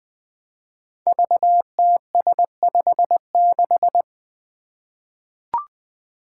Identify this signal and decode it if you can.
Morse code, heard as a keyed tone, about 20 wpm, 700 Hz. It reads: VTS56
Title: Five Challenging Letter-Number Combinations 20wpm